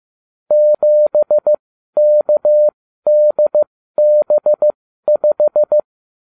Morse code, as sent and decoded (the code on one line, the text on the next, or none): --... -.- -.. -... .....
7KDB5